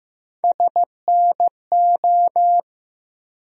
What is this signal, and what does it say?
Morse code, heard as a keyed tone, about 15 wpm, 700 Hz. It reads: SNO
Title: USA Call Sign Postfixes - 3 Letters 15wpm